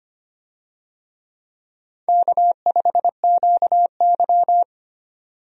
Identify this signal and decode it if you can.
Morse code, heard as a keyed tone, about 25 wpm, 700 Hz. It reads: K5QY
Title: USA Call Signs - 4 Characters - 25wpm